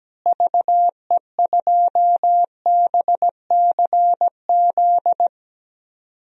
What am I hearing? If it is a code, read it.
Morse code, heard as a keyed tone, about 17 wpm, 700 Hz. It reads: VE2BCZ
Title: DXCC-100 Call Signs - 6 Characters - Repeated 2x - 17wpm